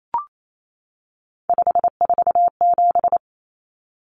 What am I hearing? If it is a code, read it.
Morse code, heard as a keyed tone, about 28 wpm, 700 Hz. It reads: 547
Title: Common Cut Numbers - 3-Digit 28wpm